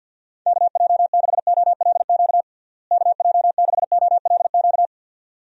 Morse code, heard as a keyed tone, about 50 wpm, 700 Hz. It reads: KJ6OLQ KJ6OLQ
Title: USA Call Signs - 6 Characters - Repeated 2x - 50wpm